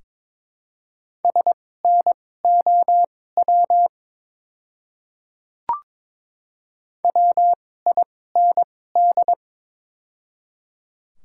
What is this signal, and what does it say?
Morse code, heard as a keyed tone, about 22 wpm, spaced as for 16 wpm, 700 Hz. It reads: SNOW WIND
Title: 2x Character Spacing - Top 500 Words - 22wpm